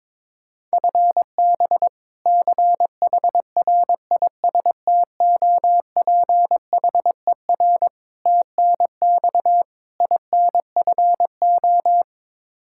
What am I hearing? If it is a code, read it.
Morse code, heard as a keyed tone, about 22 wpm, 700 Hz. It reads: FB CHRISTOPHER TNX INFO
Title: K1USN SST QSO Practice 22wpm